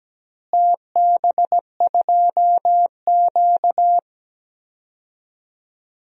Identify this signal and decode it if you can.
Morse code, heard as a keyed tone, about 17 wpm, 700 Hz. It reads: TB2Q
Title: DXCC-100 Call Signs - 4 Characters - Repeated 3x - 17wpm